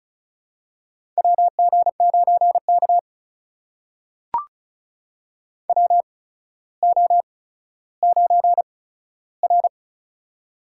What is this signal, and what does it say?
Morse code, heard as a keyed tone, about 36 wpm, spaced as for 15 wpm, 700 Hz. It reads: WG9K WO9R
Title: USA Call Signs - 4 Characters - 8x Character Spacing - 35wpm